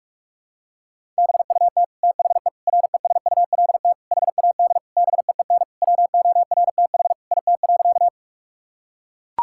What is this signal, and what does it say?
Morse code, heard as a keyed tone, about 41 wpm, 700 Hz. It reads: BUT THE RESULT HAD BEEN WORTH IT.
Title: Sentences from Top 1000 Words - Review Part 5 - 45wpm